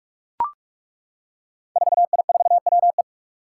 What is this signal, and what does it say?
Morse code, heard as a keyed tone, about 41 wpm, 700 Hz. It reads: 4I4WE
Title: Five Challenging Letter-Number Combinations 45wpm